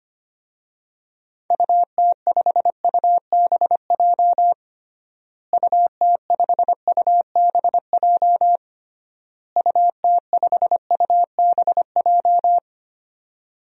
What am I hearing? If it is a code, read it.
Morse code, heard as a keyed tone, about 25 wpm, 700 Hz. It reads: UT5UBJ UT5UBJ UT5UBJ
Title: DXCC-100 Call Signs - 6 Characters - Repeated 3x - 25wpm